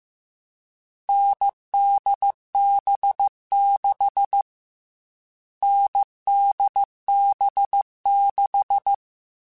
Morse code, heard as a keyed tone, about 15 wpm, 800 Hz. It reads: NDB6 NDB6
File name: Short Runs / Short Runs-mp3s / NDB NDB6 @13